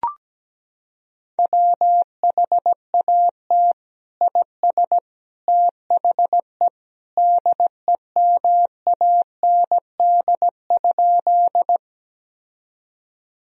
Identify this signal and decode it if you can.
Morse code, heard as a keyed tone, about 17 wpm, 700 Hz. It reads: WHAT IS THE DEMAND?
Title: Sentences from Top 1000 Words - Review Part 4 - 17wpm